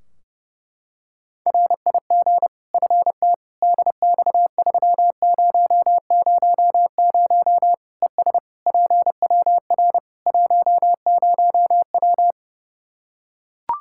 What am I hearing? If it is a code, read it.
Morse code, heard as a keyed tone, about 32 wpm, 700 Hz. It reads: RIG FT DX3000 ES PWR 10W
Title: Practice QSO - Exchange 2 - Info - 30wpm